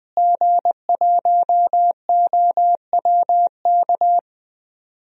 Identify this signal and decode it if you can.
Morse code, heard as a keyed tone, about 20 wpm, 700 Hz. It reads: G1OWK
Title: DXCC-100 Call Signs - 5 Characters - Repeated 3x - 20wpm